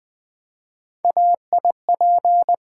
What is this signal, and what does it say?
Morse code, heard as a keyed tone, about 20 wpm, 700 Hz. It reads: AIP
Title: Three Letter-Number Combinations - 20wpm